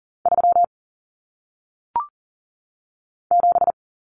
Morse code, heard as a keyed tone, about 39 wpm, 700 Hz. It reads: 3 7
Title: Numbers - 1-Digit 40wpm